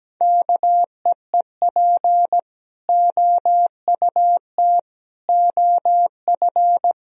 Morse code, heard as a keyed tone, about 17 wpm, 700 Hz. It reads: KEEP OUT OF
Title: Sentences from Top 1000 Words - Review Part 1 - 17wpm